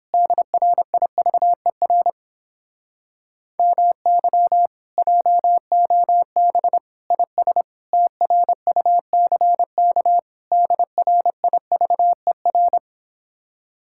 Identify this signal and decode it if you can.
Morse code, heard as a keyed tone, about 29 wpm, 700 Hz. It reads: DRIVER MY JOB IS TRUCK DRIVER